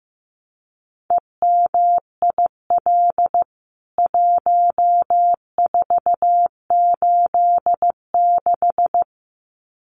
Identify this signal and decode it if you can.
Morse code, heard as a keyed tone, about 15 wpm, 700 Hz. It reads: EMIL 1486
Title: CWT Exchange 15wpm